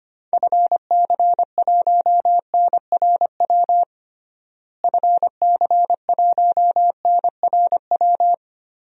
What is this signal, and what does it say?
Morse code, heard as a keyed tone, about 25 wpm, 700 Hz. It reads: FC1NRW FC1NRW
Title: DXCC-100 Call Signs - 6 Characters - Repeated 3x - 25wpm